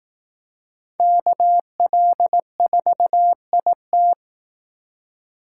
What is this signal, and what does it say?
Morse code, heard as a keyed tone, about 18 wpm, 700 Hz. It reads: KL4IT
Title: Speed Racing - USA 2x2 Call Signs 15wpm